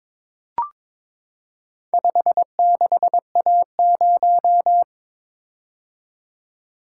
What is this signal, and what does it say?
Morse code, heard as a keyed tone, about 23 wpm, 700 Hz. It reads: 56A0
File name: practice / 22wpm / Four Letter-Number Combinations - 22wpm